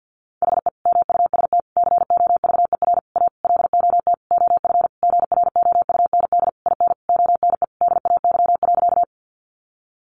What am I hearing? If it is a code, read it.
Morse code, heard as a keyed tone, about 50 wpm, 700 Hz. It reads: HE MUST COVER A LOT OF GROUND IN ONE DAY.
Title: Sentences from Top 500 Words 50wpm